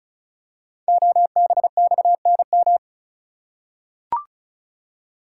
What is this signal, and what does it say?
Morse code, heard as a keyed tone, about 36 wpm, 700 Hz. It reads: OBXNM
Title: Code Groups - 5 Letters - 35wpm